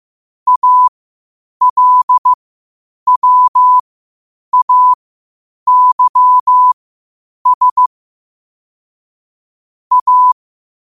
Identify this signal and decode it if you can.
Morse code, heard as a keyed tone, about 15 wpm, spaced as for 8 wpm, 1000 Hz. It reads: ALWAYS A
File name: qotdfiles / 8wpm